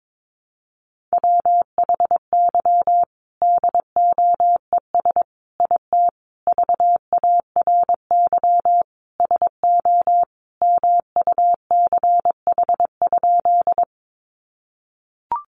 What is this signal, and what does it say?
Morse code, heard as a keyed tone, about 22 wpm, 700 Hz. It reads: WHY DOES IT VARY SO MUCH?
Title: Sentences from Top 1200 Words 22wpm